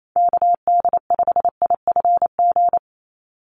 Morse code, heard as a keyed tone, about 28 wpm, 700 Hz. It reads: KD5IFG
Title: USA Call Signs 28wpm